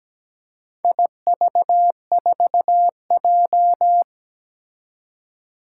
Morse code, heard as a keyed tone, about 17 wpm, 700 Hz. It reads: IV4J
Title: Four Challenging Letter-Number Combinations 17wpm